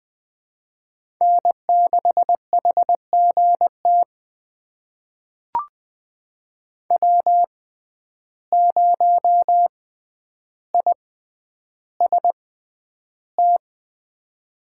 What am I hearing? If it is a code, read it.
Morse code, heard as a keyed tone, about 21 wpm, spaced as for 10 wpm, 700 Hz. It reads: N6HGT W0IST
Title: USA Call Signs - 5 Characters - 6x Character Spacing - 20wpm